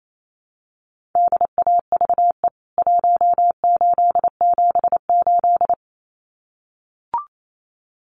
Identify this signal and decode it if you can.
Morse code, heard as a keyed tone, about 28 wpm, 700 Hz. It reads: DAVE 1878
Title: CWT Exchange 28wpm